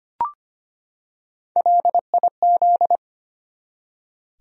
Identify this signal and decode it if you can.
Morse code, heard as a keyed tone, about 25 wpm, 700 Hz. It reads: LIZ
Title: USA Common Names - Max 4 Letters - 25wpm